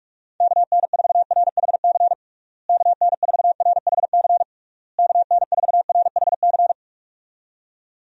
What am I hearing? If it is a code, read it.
Morse code, heard as a keyed tone, about 48 wpm, 700 Hz. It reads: KN4RHC KN4RHC KN4RHC
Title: USA Call Signs - 6 Characters - Repeated 3x - 45wpm